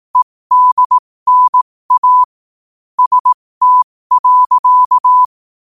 Morse code, heard as a keyed tone, about 18 wpm, spaced as for 15 wpm, 1000 Hz. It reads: EDNA ST.
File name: qotdfiles / 15wpm